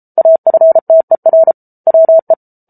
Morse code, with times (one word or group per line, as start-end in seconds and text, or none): 0.2-1.5 AFTER
1.9-2.3 WE